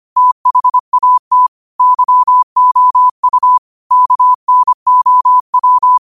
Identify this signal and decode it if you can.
Morse code, heard as a keyed tone, about 25 wpm, 1000 Hz. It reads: THAT YOU KNOW
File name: qotdfiles / 25wpm